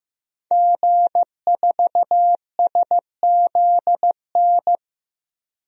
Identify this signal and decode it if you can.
Morse code, heard as a keyed tone, about 15 wpm, 700 Hz. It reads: G4SZN
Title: DXCC-100 Call Signs - 5 Characters - 15wpm